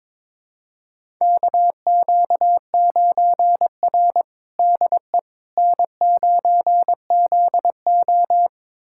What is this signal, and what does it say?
Morse code, heard as a keyed tone, about 21 wpm, 700 Hz. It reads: KQ9R DE N9ZO
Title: USA Callsign DE Callsign - 4 Characters - 22wpm